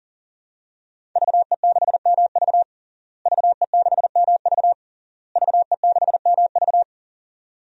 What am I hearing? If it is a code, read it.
Morse code, heard as a keyed tone, about 40 wpm, 700 Hz. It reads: VE6MV VE6MV VE6MV
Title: DXCC-100 Call Signs - 5 Characters - Repeated 3x - 40wpm